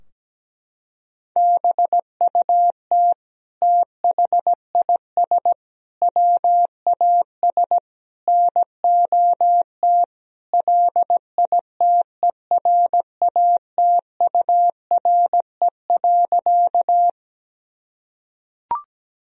Morse code, heard as a keyed tone, about 17 wpm, 700 Hz. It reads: BUT THIS WAS NOT LITERATURE.
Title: Sentences from Top 1000 Words - Review Part 3 - 17wpm